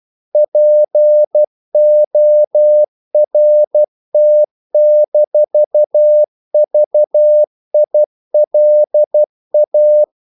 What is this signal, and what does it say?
Morse code, heard as a keyed tone, about 12 wpm, 600 Hz. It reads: PORT-VILA
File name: CW / Practice / NationalCapitals12